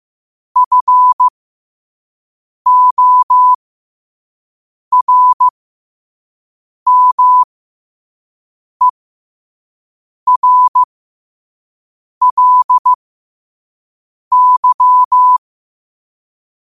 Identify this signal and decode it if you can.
Morse code, heard as a keyed tone, about 15 wpm, spaced as for 5 wpm, 1000 Hz. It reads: FORMERLY
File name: qotdfiles / 5wpm